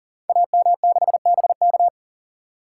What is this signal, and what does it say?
Morse code, heard as a keyed tone, about 41 wpm, 700 Hz. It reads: AM6BK
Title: Code Groups - 5 Letter:Numbers - 40wpm